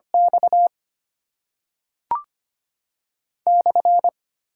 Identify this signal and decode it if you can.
Morse code, heard as a keyed tone, about 25 wpm, 700 Hz. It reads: X /
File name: Top 90 QSO Elements - 8x Character Spacing - 25wpm